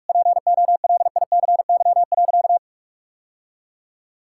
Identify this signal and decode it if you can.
Morse code, heard as a keyed tone, about 41 wpm, 700 Hz. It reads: POLICY.
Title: Sentences from Top 1000 Words - Review Part 5 - 45wpm